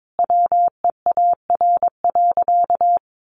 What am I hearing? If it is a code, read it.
Morse code, heard as a keyed tone, about 22 wpm, 700 Hz. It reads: WEAR.
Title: Sentences from Top 1500 Words 22wpm